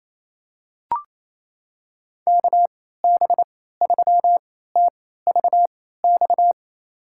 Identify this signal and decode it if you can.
Morse code, heard as a keyed tone, about 28 wpm, spaced as for 16 wpm, 700 Hz. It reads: KB3TVX
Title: USA Call Signs - 6 Characters - 3x Character Spacing - 28wpm